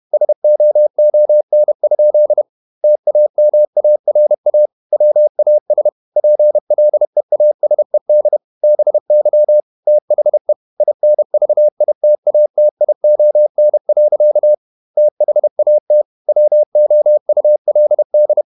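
Morse code, heard as a keyed tone, about 31 wpm, 600 Hz. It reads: SOON? TAMARA WAS PLEASED BY THE INVITATION. THAT WOULD
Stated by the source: Morse code at 30 wpm, the novel Never by Ken Follett in English, from CW med Gnister.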